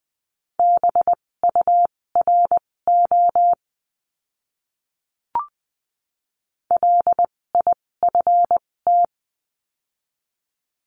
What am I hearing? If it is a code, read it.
Morse code, heard as a keyed tone, about 20 wpm, spaced as for 16 wpm, 700 Hz. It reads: BURO LIFT
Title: 256 Lesson 42a - Words 12-20wpm